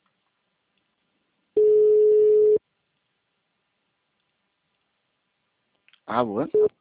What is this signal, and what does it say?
Ringback tone (Europe, 425 Hz), with speech.